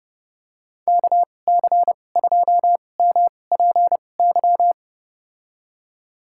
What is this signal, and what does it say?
Morse code, heard as a keyed tone, about 30 wpm, spaced as for 22 wpm, 700 Hz. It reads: KC2MPY
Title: USA Call Signs - 6 Characters - 2x Character Spacing - 30wpm